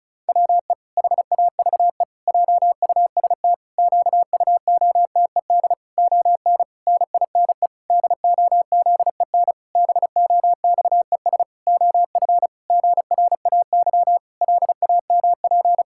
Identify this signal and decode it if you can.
Morse code, heard as a keyed tone, about 36 wpm, 700 Hz. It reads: WE HAVE JUST QUOTED ON NINE DOZEN BOXES OF GRAY LAMP
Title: Pangram Challenge v2 - 35wpm